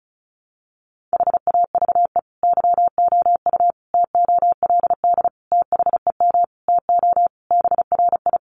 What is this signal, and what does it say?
Morse code, heard as a keyed tone, about 34 wpm, 700 Hz. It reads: HAVE YOU TOLD THEM TO BRI
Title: Sentences from Top 500 Words 35wpm